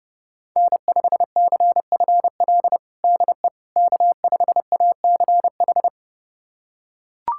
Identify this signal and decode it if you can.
Morse code, heard as a keyed tone, about 30 wpm, 700 Hz. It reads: N5CFL DE K5ACH
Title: USA Callsign DE Callsign - 5 Characters - 30wpm